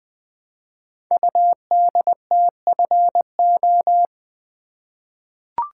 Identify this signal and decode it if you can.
Morse code, heard as a keyed tone, about 20 wpm, 700 Hz. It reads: UDTFO
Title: Code Groups - 5 Letters - 20wpm